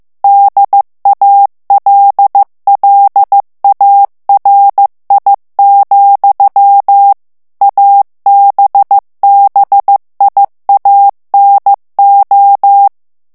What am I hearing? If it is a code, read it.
Morse code, heard as a keyed tone, about 15 wpm, 800 Hz. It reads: DALLARI, ABBIANO
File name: morse mp3 / ITALIEN texte clair 2 vitesse 14 mots minute